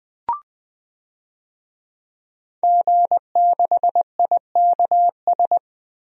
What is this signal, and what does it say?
Morse code, heard as a keyed tone, about 21 wpm, 700 Hz. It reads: G6IKS
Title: DXCC-100 Call Signs - 5 Characters - Repeated 2x - 20wpm